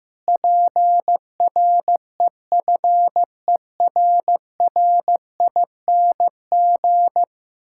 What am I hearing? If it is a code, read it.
Morse code, heard as a keyed tone, about 15 wpm, 700 Hz. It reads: PREFERRING
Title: Pangram Challenge v2 - 15wpm